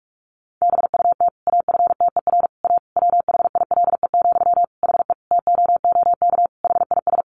Morse code, heard as a keyed tone, about 44 wpm, 700 Hz. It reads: BUT AFTER A WHILE, HE TOOK HIS
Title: Sentences from Top 1500 Words 45wpm